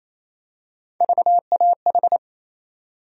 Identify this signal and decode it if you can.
Morse code, heard as a keyed tone, about 28 wpm, 700 Hz. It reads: VAH
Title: USA Call Sign Postfixes - 3 Letters 28wpm